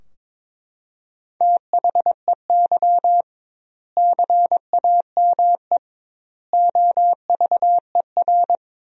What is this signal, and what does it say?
Morse code, heard as a keyed tone, about 22 wpm, 700 Hz. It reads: THEY CAME OVER
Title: 2x Word-Spacing - Sentences from Top 300 Words - 22wpm